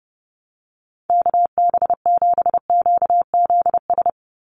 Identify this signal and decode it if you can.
Morse code, heard as a keyed tone, about 28 wpm, 700 Hz. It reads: KB7QZS
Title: USA Call Signs 30wpm